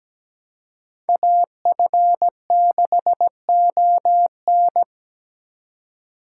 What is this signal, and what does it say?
Morse code, heard as a keyed tone, about 17 wpm, 700 Hz. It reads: AF6ON
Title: USA Call Signs - 2x Character Spacing - 17wpm